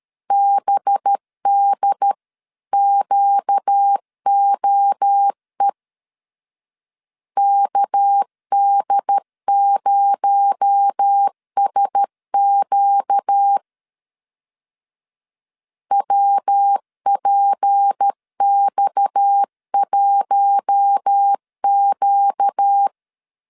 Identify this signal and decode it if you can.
Morse code, heard as a keyed tone, about 13 wpm, 800 Hz. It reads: BDQOE KD0SQ WPX1Q
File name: CW / DET / 1 6 1500C S B 12wpm 3431